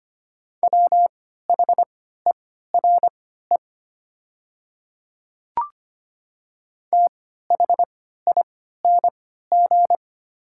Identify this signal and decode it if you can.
Morse code, heard as a keyed tone, about 25 wpm, spaced as for 14 wpm, 700 Hz. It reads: WHERE THING
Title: 3x Character Spacing - Top 200 Words - 25wpm